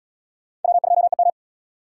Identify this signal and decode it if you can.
Morse code, heard as a keyed tone, about 72 wpm, 700 Hz. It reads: W2EM